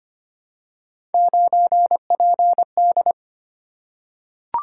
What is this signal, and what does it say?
Morse code, heard as a keyed tone, about 25 wpm, 700 Hz. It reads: 9PD
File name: Three Letter-Number Combinations - 25wpm